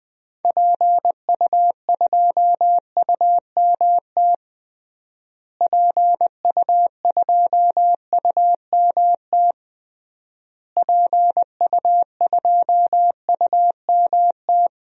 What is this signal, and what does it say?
Morse code, heard as a keyed tone, about 20 wpm, 700 Hz. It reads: PU2UMT PU2UMT PU2UMT
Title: DXCC-100 Call Signs - 6 Characters - Repeated 3x - 20wpm